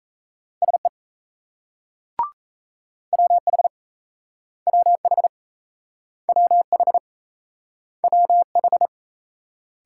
Morse code, heard as a keyed tone, about 36 wpm, 700 Hz. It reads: SE WH WH WH WH